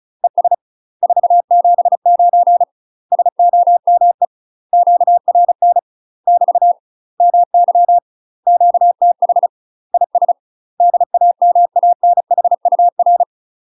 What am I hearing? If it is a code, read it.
Morse code, heard as a keyed tone, about 33 wpm, 700 Hz. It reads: ES 479 SOME QRN <BT> MY QTH IS DAMANHUR
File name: UK MASTER QSOs TRAINING / 35WPM / UK MASTER QSOs 0013